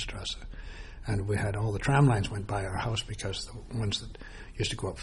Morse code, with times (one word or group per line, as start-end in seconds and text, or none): none